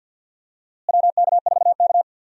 Morse code, heard as a keyed tone, about 52 wpm, 700 Hz. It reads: WK4K
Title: USA Call Signs - 4 Characters - Repeated 3x - 50wpm